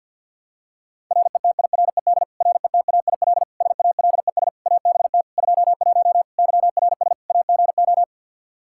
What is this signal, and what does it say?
Morse code, heard as a keyed tone, about 51 wpm, 700 Hz. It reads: RETIRED RETAIL SALES ABT 21 YRS AGO
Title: Practice QSO - Exchange 3 - More Info - 50wpm